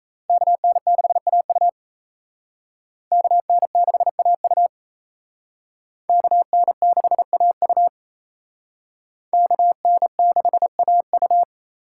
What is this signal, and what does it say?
Morse code, heard as a keyed tone, about 35 wpm, 700 Hz. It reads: KN6AU KN6AU KN6AU KN6AU
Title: Speed Racing - USA 2x2 Call Signs 28wpm